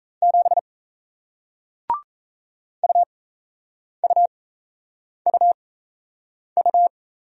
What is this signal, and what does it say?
Morse code, heard as a keyed tone, about 38 wpm, spaced as for 8 wpm, 700 Hz. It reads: 7 UUUU